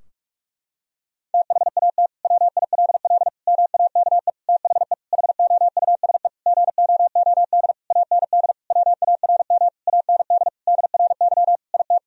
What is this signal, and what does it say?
Morse code, heard as a keyed tone, about 41 wpm, 700 Hz. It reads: THAT WILL MAKE THE HOUSE GOOD AND WARM AND DRY IT
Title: Sentences from Top 1000 Words - Review Part 3 - 45wpm